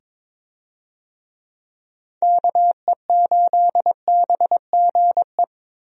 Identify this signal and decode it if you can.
Morse code, heard as a keyed tone, about 21 wpm, 700 Hz. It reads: KE8BGE
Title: USA Call Signs - 6 Characters - 22wpm